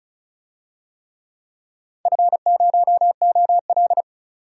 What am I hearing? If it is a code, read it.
Morse code, heard as a keyed tone, about 36 wpm, 700 Hz. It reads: F0OL
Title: DXCC-100 Call Signs - 4 Characters - 35wpm